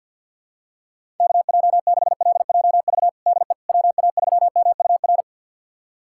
Morse code, heard as a keyed tone, about 48 wpm, 700 Hz. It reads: KJ6LJV DE WA3MUR
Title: USA Callsign DE Callsign - 6 Characters - 50wpm